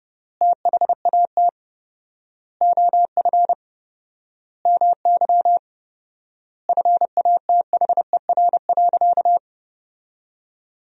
Morse code, heard as a keyed tone, about 30 wpm, 700 Hz. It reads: THAT OF MY FATHER.
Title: 4x Word-Spacing - Sentences from Top 500 Words - 30wpm